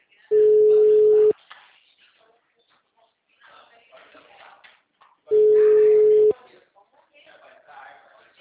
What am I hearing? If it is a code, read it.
Ringback tone (Europe, 425 Hz), with speech.